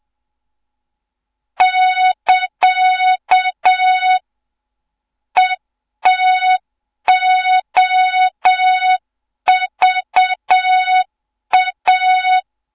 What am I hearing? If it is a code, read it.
Morse code, heard as a keyed tone, about 7 wpm, 3700 Hz. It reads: <KA> ETOVA
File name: Morze-HA2MN / hang / 07 GYAKORLAT